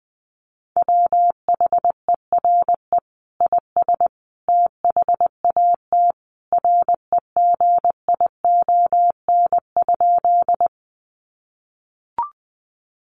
Morse code, heard as a keyed tone, about 20 wpm, 700 Hz. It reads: WHERE IS THAT REGION?
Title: Sentences from Top 1500 Words 20wpm